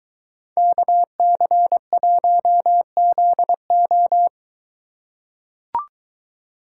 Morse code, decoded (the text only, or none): KC1ZO